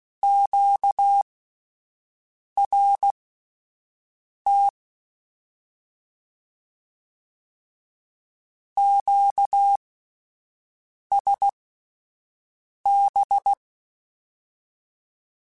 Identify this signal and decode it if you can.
Morse code, heard as a keyed tone, about 16 wpm, spaced as for 5 wpm, 800 Hz. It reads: QRT QSB